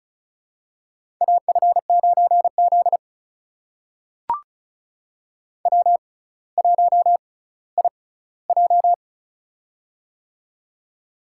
Morse code, heard as a keyed tone, about 36 wpm, spaced as for 18 wpm, 700 Hz. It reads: AF9Z W1IJ
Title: USA Call Signs - 4 Characters - 6x Character Spacing - 35wpm